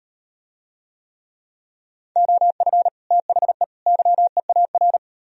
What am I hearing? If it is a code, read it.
Morse code, heard as a keyed tone, about 38 wpm, 700 Hz. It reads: OF THE YEAR